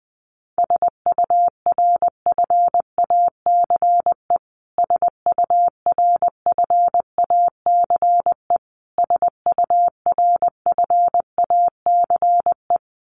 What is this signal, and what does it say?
Morse code, heard as a keyed tone, about 20 wpm, 700 Hz. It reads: SURFACE SURFACE SURFACE
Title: Top 500 Words Repeated 20wpm